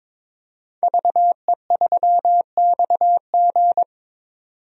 Morse code, heard as a keyed tone, about 21 wpm, 700 Hz. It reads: VE3XG
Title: DXCC-100 Call Signs - 5 Characters - 22wpm